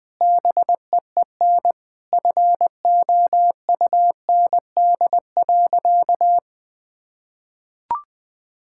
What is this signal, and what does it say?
Morse code, heard as a keyed tone, about 20 wpm, 700 Hz. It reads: BEEN FOUND.
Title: Sentences from Top 1600 Words 20wpm